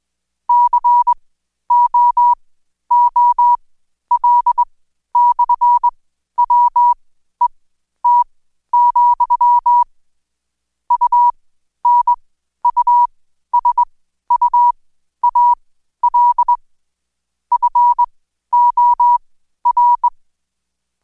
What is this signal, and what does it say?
Morse code, heard as a keyed tone, about 21 wpm, spaced as for 12 wpm, 970 Hz. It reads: COOL/WET, UNUSUAL FOR
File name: Code Quick / AudioDownld2 / Seg121-Day26-DayDrill 26